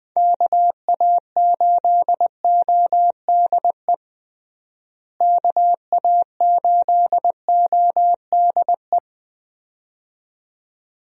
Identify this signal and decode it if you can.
Morse code, heard as a keyed tone, about 20 wpm, 700 Hz. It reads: KA8ODE KA8ODE
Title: USA Call Signs - 6 Characters - Repeated 2x - 20wpm